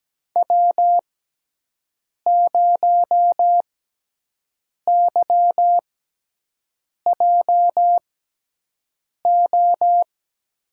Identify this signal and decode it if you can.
Morse code, heard as a keyed tone, about 17 wpm, spaced as for 6 wpm, 700 Hz. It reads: W0YJO
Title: USA Call Signs - 5 Characters - 6x Character Spacing - 17wpm